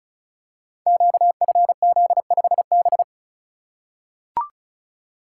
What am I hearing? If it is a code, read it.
Morse code, heard as a keyed tone, about 36 wpm, 700 Hz. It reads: QFZ5B
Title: Code Groups - 5 Letter:Numbers - 35wpm